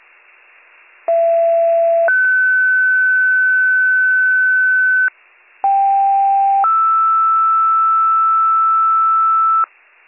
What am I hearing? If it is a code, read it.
Two-tone page, heard as a one-second tone, then a three-second tone, about 670 then 1500 Hz, about 790 then 1320 Hz, with noise. It is heard from 1.1 to 9.6 s.